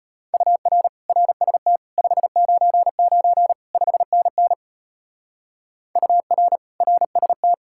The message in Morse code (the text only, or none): UR RST 599 5NN UR RST